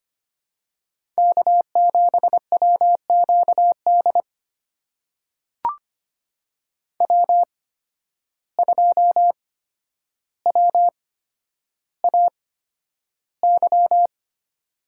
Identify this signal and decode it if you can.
Morse code, heard as a keyed tone, about 25 wpm, spaced as for 11 wpm, 700 Hz. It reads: K7WQD W2WAY
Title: USA Call Signs - 8x Character Spacing - 25wpm